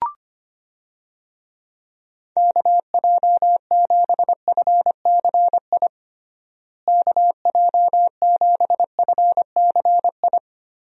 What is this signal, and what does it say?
Morse code, heard as a keyed tone, about 25 wpm, 700 Hz. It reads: KJ7FCI KJ7FCI
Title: USA Call Signs - 6 Characters - Repeated 3x - 25wpm